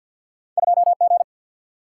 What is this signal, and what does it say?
Morse code, heard as a keyed tone, about 49 wpm, 700 Hz. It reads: 2G
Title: Single-Letter Single-Number 50wpm